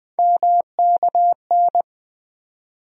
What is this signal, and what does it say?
Morse code, heard as a keyed tone, about 20 wpm, 700 Hz. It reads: MKN
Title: USA Call Sign Postfixes - 3 Letters 20wpm